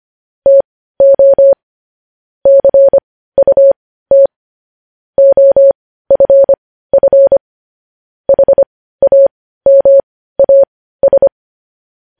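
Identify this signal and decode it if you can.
Morse code, heard as a keyed tone, about 25 wpm, spaced as for 15 wpm, 550 Hz. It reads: TO CUT OFF HAMAS
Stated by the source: Morse code news headlines at 15 WPM.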